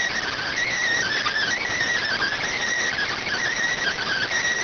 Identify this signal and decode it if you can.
Emergency Alert System (SAME) tones, heard as harsh buzzing bursts. It is heard from 0.3 to 4.3 s.